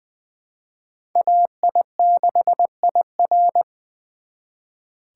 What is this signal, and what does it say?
Morse code, heard as a keyed tone, about 23 wpm, 700 Hz. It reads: AI6IR
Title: Speed Racing - USA 2x2 Call Signs 20wpm